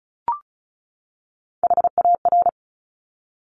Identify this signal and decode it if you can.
Morse code, heard as a keyed tone, about 36 wpm, 700 Hz. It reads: HAR
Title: Top Three Letter Combinations in Words 35wpm